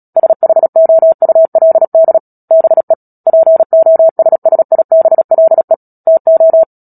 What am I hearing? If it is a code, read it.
Morse code, heard as a keyed tone, about 37 wpm, 650 Hz. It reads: SHOULD BE POSSIBLE TO
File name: Book Two / 10-Chapter 10 - Epilogue - 36 WPM